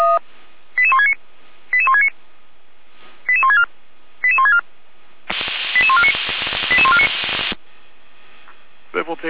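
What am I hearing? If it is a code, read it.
Five-tone selective call (ZVEI), heard as quick runs of five tones, tones of about 70 ms. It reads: 88169 88169 88164 88164 88149 88149
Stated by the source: Received by SDR, ZVEI five-tone call 88159.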